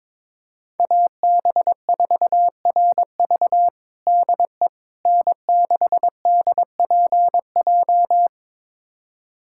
Morse code, heard as a keyed tone, about 22 wpm, 700 Hz. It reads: AB4RV DE N6DPJ
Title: USA Callsign DE Callsign - 5 Characters - 22wpm